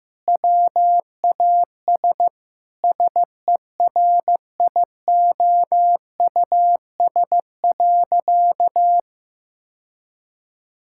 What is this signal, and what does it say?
Morse code, heard as a keyed tone, about 15 wpm, 700 Hz. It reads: WAS SERIOUS.
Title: Sentences from Top 1600 Words 15wpm